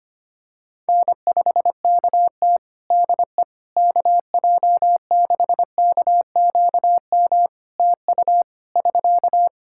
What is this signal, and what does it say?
Morse code, heard as a keyed tone, about 25 wpm, 700 Hz. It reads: N5KT DE KJ6KQM TU <SK>
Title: Practice QSO - Exchange 4 - Ending - 25wpm